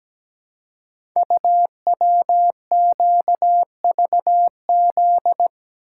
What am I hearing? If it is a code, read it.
Morse code, heard as a keyed tone, about 17 wpm, 700 Hz. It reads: UWQVZ
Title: Code Groups - 5 Letters - 17wpm